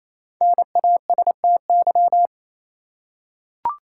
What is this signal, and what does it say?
Morse code, heard as a keyed tone, about 28 wpm, 700 Hz. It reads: NASTY
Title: Five Letter Words - 28wpm